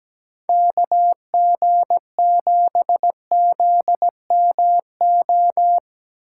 Morse code, heard as a keyed tone, about 17 wpm, 700 Hz. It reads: KG7ZMO
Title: USA Call Signs - 2x Character Spacing - 17wpm